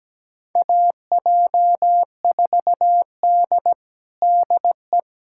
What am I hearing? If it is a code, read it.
Morse code, heard as a keyed tone, about 17 wpm, 700 Hz. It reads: AJ4D DE
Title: USA Callsign DE Callsign - 4 Characters - 17wpm